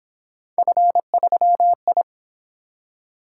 Morse code, heard as a keyed tone, about 28 wpm, 700 Hz. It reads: F3I